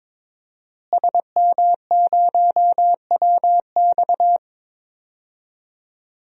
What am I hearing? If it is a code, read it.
Morse code, heard as a keyed tone, about 22 wpm, 700 Hz. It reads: SM0WX
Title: DXCC-100 Call Signs - 5 Characters - Repeated 2x - 22wpm